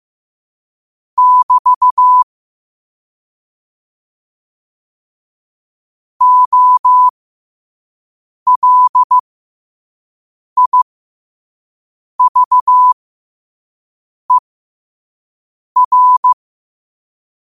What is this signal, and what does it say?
Morse code, heard as a keyed tone, about 15 wpm, spaced as for 5 wpm, 1000 Hz. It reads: <BT> OLIVER
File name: qotdfiles / 5wpm